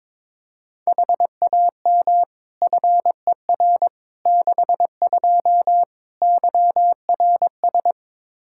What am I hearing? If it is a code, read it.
Morse code, heard as a keyed tone, about 22 wpm, 700 Hz. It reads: HAM FER 62 YRS
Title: Practice QSO - Exchange 3 - More Info - 22wpm